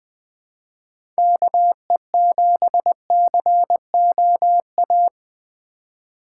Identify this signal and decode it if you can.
Morse code, heard as a keyed tone, about 21 wpm, 700 Hz. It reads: KE7COA
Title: USA Call Signs - 4x Character Spacing - 20wpm